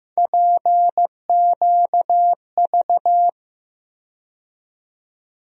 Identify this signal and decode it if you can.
Morse code, heard as a keyed tone, about 15 wpm, 700 Hz. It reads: PQV
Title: USA Call Sign Postfixes - 3 Letters 15wpm